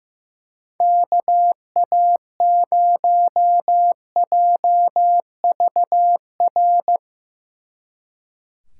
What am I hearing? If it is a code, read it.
Morse code, heard as a keyed tone, about 17 wpm, 700 Hz. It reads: KA0JVR
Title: Speed Racing - USA 2x3 Call Signs 15wpm